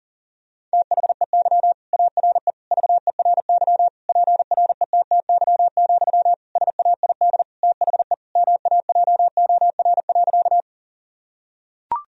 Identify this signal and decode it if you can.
Morse code, heard as a keyed tone, about 40 wpm, 700 Hz. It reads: THEY ARE VERY PRETTY, SAID THE MAJOR.
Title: Sentences from Top 1000 Words - Review Part 4 - 40wpm